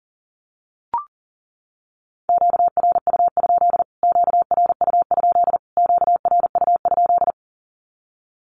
Morse code, heard as a keyed tone, about 40 wpm, 700 Hz. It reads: QRU? QRU? QRU?
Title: Q-Codes Repeated 3x 40wpm